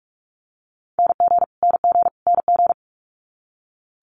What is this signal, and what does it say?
Morse code, heard as a keyed tone, about 46 wpm, 700 Hz. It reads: NG NG NG
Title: Multi-letter Phonemes Repeated 3x 45wpm